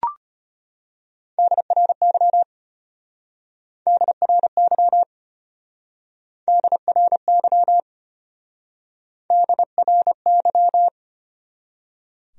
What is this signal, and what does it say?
Morse code, heard as a keyed tone, about 33 wpm, 700 Hz. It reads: DRY DRY DRY DRY